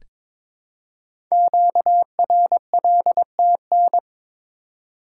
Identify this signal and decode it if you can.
Morse code, heard as a keyed tone, about 22 wpm, 700 Hz. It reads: QRLTN
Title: Code Groups - 5 Letters - 22wpm